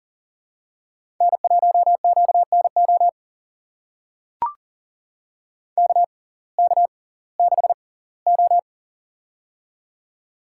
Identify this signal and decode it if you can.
Morse code, heard as a keyed tone, about 40 wpm, spaced as for 22 wpm, 700 Hz. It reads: N1QNO KK6O
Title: USA Call Signs - 6x Character Spacing - 40wpm